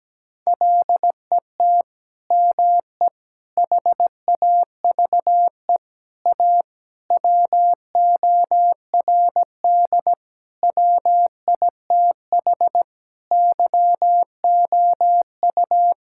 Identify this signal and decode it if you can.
Morse code, heard as a keyed tone, about 17 wpm, 700 Hz. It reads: LET ME HAVE A WORD WITH YOU
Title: Sentences from Top 1000 Words - Review Part 6 - 17wpm